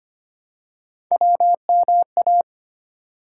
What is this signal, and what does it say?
Morse code, heard as a keyed tone, about 25 wpm, 700 Hz. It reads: WMA
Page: morse-code-ninja-practice-25wpm